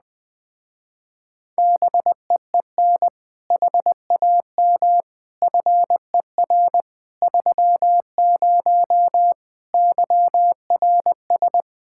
Morse code, heard as a keyed tone, about 20 wpm, 700 Hz. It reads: BEEN HAM FER 30 YRS
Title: Speed-Racing Common QSO Phrases Encore - 17wpm